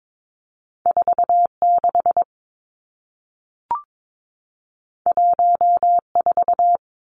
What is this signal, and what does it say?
Morse code, heard as a keyed tone, about 22 wpm, 700 Hz. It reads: 4 6 1 4
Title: Two Challenging Letter-Number Combinations 22wpm